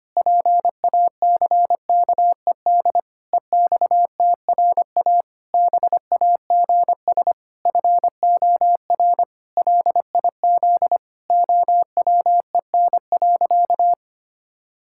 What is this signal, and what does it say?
Morse code, heard as a keyed tone, about 25 wpm, 700 Hz. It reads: PACKED EXTRA BAGS FOR LIZ OWEN.
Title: Pangram Challenge v2 - 25wpm